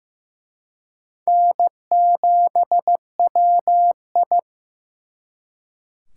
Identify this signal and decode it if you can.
Morse code, heard as a keyed tone, about 18 wpm, 700 Hz. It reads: N7WI
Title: Speed Racing - USA 1x2 Call Signs 15wpm